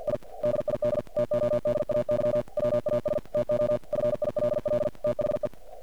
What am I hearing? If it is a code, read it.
Morse code, heard as a keyed tone, about 48 wpm, 590 Hz, with noise. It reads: I DID TODAY WAS TO FILL THE
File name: archived / CRSnet